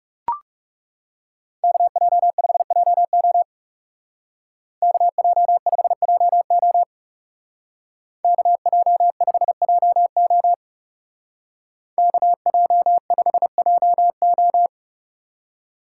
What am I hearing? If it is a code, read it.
Morse code, heard as a keyed tone, about 37 wpm, 700 Hz. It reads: KJ5JO KJ5JO KJ5JO KJ5JO
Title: Speed Racing - USA 2x2 Call Signs 30wpm